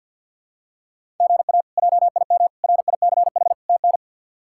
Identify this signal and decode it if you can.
Morse code, heard as a keyed tone, about 50 wpm, 700 Hz. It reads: GA JIM RICH TN
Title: K1USN SST QSO Practice - 50wpm